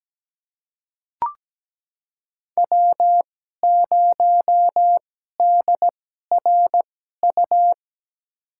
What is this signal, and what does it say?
Morse code, heard as a keyed tone, about 17 wpm, spaced as for 12 wpm, 700 Hz. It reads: W0DRU
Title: USA Call Signs - 5 Characters - 2x Character Spacing - 17wpm